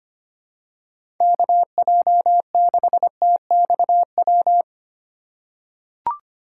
Morse code, heard as a keyed tone, about 25 wpm, 700 Hz. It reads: KJ6TXW
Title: USA Call Signs - 6x Character Spacing - 25wpm